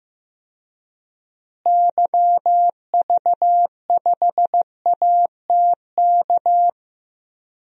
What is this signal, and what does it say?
Morse code, heard as a keyed tone, about 15 wpm, 700 Hz. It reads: YV5ATK
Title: DXCC-100 Call Signs - 6 Characters - Repeated 3x - 15wpm